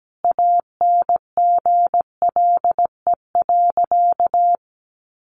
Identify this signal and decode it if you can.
Morse code, heard as a keyed tone, about 17 wpm, 700 Hz. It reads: ANGLE.
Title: Sentences from Top 1500 Words 17wpm